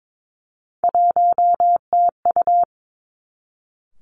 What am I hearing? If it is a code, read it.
Morse code, heard as a keyed tone, about 22 wpm, 700 Hz. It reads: 1TU
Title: Three Challenging Letter-Number Combinations 22wpm